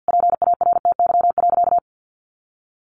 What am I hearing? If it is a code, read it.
Morse code, heard as a keyed tone, about 50 wpm, 700 Hz. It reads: PARTY.
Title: Sentences from Top 1500 Words 50wpm